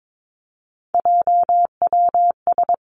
Morse code, heard as a keyed tone, about 22 wpm, 700 Hz. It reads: JWS
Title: Three Challenging Letter-Number Combinations 22wpm